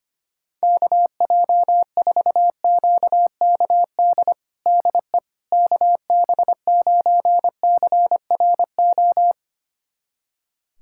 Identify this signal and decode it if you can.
Morse code, heard as a keyed tone, about 25 wpm, 700 Hz. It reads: KJ4QKD DE KB9CRO
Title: USA Callsign DE Callsign - 6 Characters - 25wpm